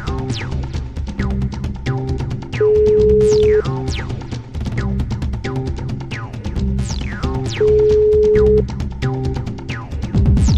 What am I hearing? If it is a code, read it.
Ringback tone (Europe, 425 Hz), with noise.